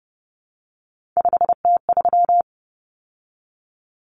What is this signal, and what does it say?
Morse code, heard as a keyed tone, about 30 wpm, 700 Hz. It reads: 5T3
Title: Common Cut Numbers - 3-Digit 30wpm